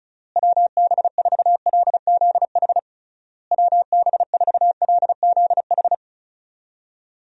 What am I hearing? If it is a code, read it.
Morse code, heard as a keyed tone, about 34 wpm, 700 Hz. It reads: WB4LZH WB4LZH
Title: USA Call Signs - 6 Characters - Repeated 3x - 35wpm